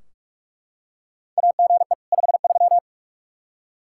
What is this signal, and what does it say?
Morse code, heard as a keyed tone, about 41 wpm, 700 Hz. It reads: AGE 53
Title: Practice QSO - Exchange 3 - More Info - 45wpm